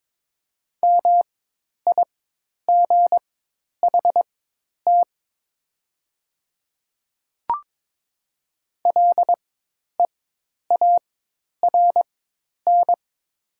Morse code, heard as a keyed tone, about 22 wpm, spaced as for 10 wpm, 700 Hz. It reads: MIGHT LEARN
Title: 4x Character Spacing - Top 500 Words - 22wpm